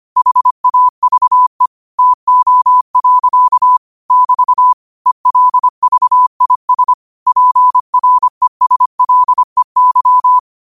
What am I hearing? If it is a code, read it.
Morse code, heard as a keyed tone, about 25 wpm, 1000 Hz. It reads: HAVE TO. <BT> ELVIS PRESLEY
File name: qotdfiles / 25wpm